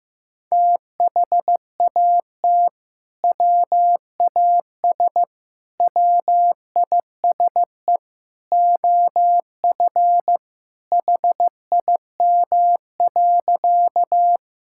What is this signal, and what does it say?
Morse code, heard as a keyed tone, about 15 wpm, 700 Hz. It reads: THAT WAS WISE OF HIM.
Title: Sentences from Top 1600 Words 15wpm